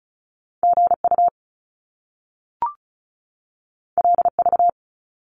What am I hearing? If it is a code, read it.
Morse code, heard as a keyed tone, about 33 wpm, 700 Hz. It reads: GU LV
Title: Top Two Letter Combinations in Words 35wpm